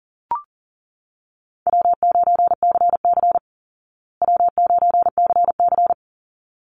Morse code, heard as a keyed tone, about 40 wpm, 700 Hz. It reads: W9CC W9CC
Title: Easy Call Signs Repeated 2x - 40wpm